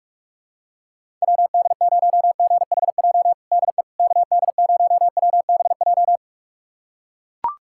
Morse code, heard as a keyed tone, about 45 wpm, 700 Hz. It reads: WD0GHJ DE KD0WBJ
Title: USA Callsign DE Callsign - 6 Characters - 45wpm